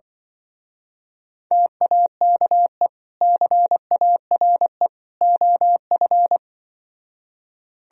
Morse code, heard as a keyed tone, about 25 wpm, 700 Hz. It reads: TAKE CARE OF
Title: Speed-Racing - Sets of 3 Words 20wpm